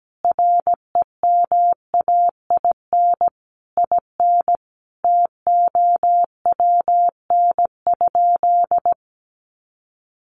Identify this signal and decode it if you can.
Morse code, heard as a keyed tone, about 17 wpm, 700 Hz. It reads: REMAIN IN TOWN?
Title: Sentences from Top 1200 Words 17wpm